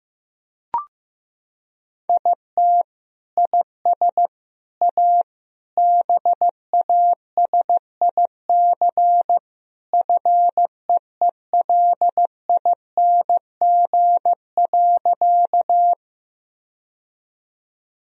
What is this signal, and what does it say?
Morse code, heard as a keyed tone, about 15 wpm, 700 Hz. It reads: IT IS A BASIC FEELING.
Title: Sentences from Top 1000 Words - Review Part 3 - 15wpm